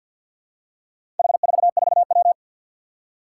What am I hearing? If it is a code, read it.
Morse code, heard as a keyed tone, about 52 wpm, 700 Hz. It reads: H44W
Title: Four Challenging Letter-Number Combinations 50wpm